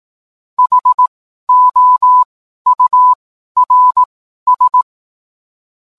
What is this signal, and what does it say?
Morse code, heard as a keyed tone, about 18 wpm, spaced as for 12 wpm, 1000 Hz. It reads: HOURS